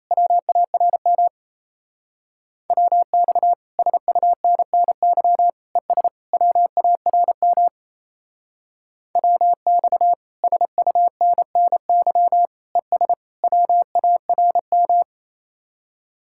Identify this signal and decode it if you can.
Morse code, heard as a keyed tone, about 35 wpm, 700 Hz. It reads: WARM WX SUNNY ES WARM WX SUNNY ES WARM
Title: Speed-Racing Common QSO Phrases - 28wpm